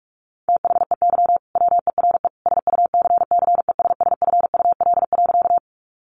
Morse code, heard as a keyed tone, about 44 wpm, 700 Hz. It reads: THEY WERE SUCCESSFUL.
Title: Sentences from Top 900 Words 45wpm